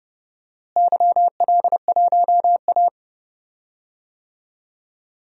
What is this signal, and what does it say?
Morse code, heard as a keyed tone, about 31 wpm, 700 Hz. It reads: YL1A
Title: DXCC-100 Call Signs - 4 Characters - Repeated 2x - 30wpm